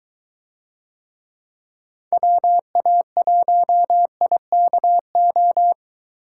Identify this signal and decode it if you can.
Morse code, heard as a keyed tone, about 23 wpm, 700 Hz. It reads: WA1IKO